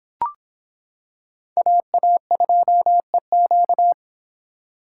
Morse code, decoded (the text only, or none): AA2EQ